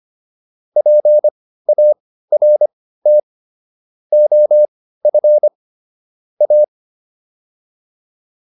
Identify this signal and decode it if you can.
Morse code, heard as a keyed tone, about 25 wpm, spaced as for 15 wpm, 600 Hz. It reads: PART OF A